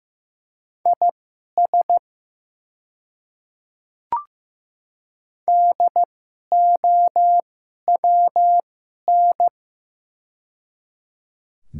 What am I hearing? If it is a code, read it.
Morse code, heard as a keyed tone, about 15 wpm, spaced as for 11 wpm, 700 Hz. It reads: IS DOWN
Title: 2x Character Spacing - Top 300 Words - 15wpm